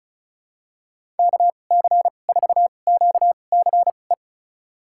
Morse code, heard as a keyed tone, about 34 wpm, spaced as for 25 wpm, 700 Hz. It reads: KC4QCE
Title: USA Call Signs - 6 Characters - 2x Character Spacing - 35wpm